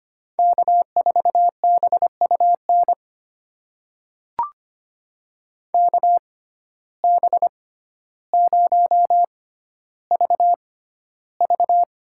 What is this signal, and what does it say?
Morse code, heard as a keyed tone, about 25 wpm, spaced as for 13 wpm, 700 Hz. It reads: K4BUN KB0VV
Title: USA Call Signs - 5 Characters - 6x Character Spacing - 25wpm